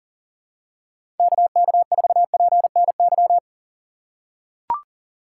Morse code, heard as a keyed tone, about 38 wpm, 700 Hz. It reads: KK4PNY
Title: USA Call Signs - 6 Characters - 6x Character Spacing - 40wpm